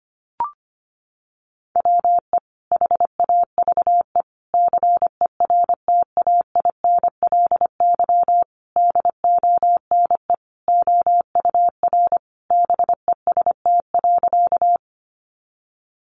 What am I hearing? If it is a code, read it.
Morse code, heard as a keyed tone, about 25 wpm, 700 Hz. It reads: WE HAVE CERTAINLY DONE OUR BEST.
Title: Sentences from Top 900 Words 25wpm